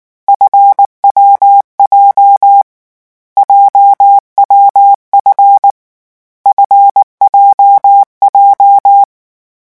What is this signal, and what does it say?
Morse code, heard as a keyed tone, about 19 wpm, 800 Hz. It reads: FWJ JWF FJJ
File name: morse mp3 / lecon7-vitesse 18